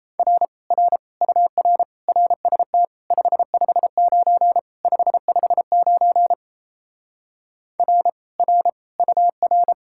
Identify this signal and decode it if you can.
Morse code, heard as a keyed tone, about 37 wpm, 700 Hz. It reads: R R UR RST 559 559 R R UR